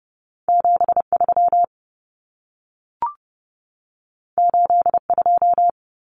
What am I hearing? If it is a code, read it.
Morse code, heard as a keyed tone, about 30 wpm, 700 Hz. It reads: 73 82
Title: Common Cut Numbers - 2-Digit 30wpm